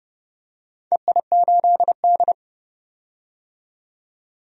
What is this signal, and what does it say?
Morse code, heard as a keyed tone, about 30 wpm, 700 Hz. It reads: EI8D
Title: DXCC-100 Call Signs - 4 Characters - 30wpm